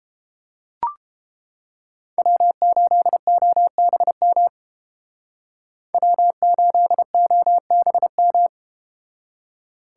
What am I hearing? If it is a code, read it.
Morse code, heard as a keyed tone, about 30 wpm, 700 Hz. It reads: W8OBM W8OBM